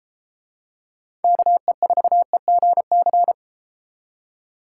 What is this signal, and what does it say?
Morse code, heard as a keyed tone, about 32 wpm, 700 Hz. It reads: KE4EGC